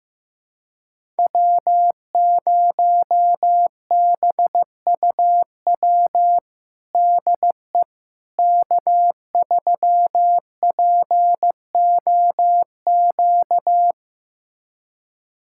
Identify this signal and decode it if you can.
Morse code, heard as a keyed tone, about 15 wpm, 700 Hz. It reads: W0BUW DE K3POQ
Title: USA Callsign DE Callsign - 5 Characters - 15wpm